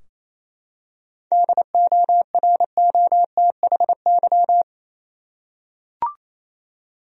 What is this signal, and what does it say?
Morse code, heard as a keyed tone, about 28 wpm, 700 Hz. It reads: DOROTHY
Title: USA Common Names - 28wpm